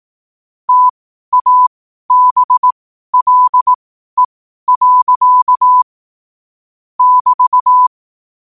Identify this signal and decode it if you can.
Morse code, heard as a keyed tone, about 18 wpm, spaced as for 12 wpm, 1000 Hz. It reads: TABLE. <BT>
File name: qotdfiles / 12wpm